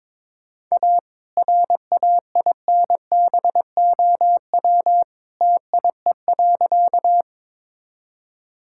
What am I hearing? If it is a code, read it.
Morse code, heard as a keyed tone, about 21 wpm, 700 Hz. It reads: A RAINBOW TIE.
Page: morse-code-ninja-practice-22wpm